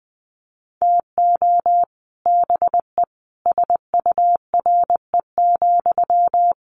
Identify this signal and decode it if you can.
Morse code, heard as a keyed tone, about 20 wpm, 700 Hz. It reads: TO BE SURE,
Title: Sentences from Top 600 Words 20wpm